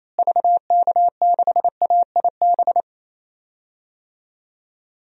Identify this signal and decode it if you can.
Morse code, heard as a keyed tone, about 28 wpm, 700 Hz. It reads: VK6AIB
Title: DXCC-100 Call Signs - 6 Characters - Repeated 2x - 28wpm